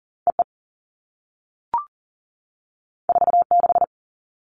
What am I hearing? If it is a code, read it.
Morse code, heard as a keyed tone, about 40 wpm, 700 Hz. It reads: EE 46
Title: All Cut Numbers - 2-Digit 40wpm